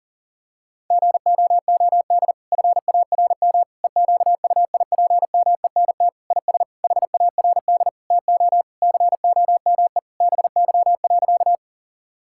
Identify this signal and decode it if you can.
Morse code, heard as a keyed tone, about 40 wpm, 700 Hz. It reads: GOOD FARM EQUIPMENT IS HARD TO COME BY.
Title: Sentences from Top 1000 Words - Review Part 5 - 40wpm